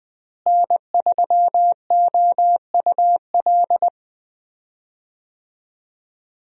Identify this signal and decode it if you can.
Morse code, heard as a keyed tone, about 20 wpm, 700 Hz. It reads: N3OUL
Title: USA Call Signs - 5 Characters - Repeated 3x - 20wpm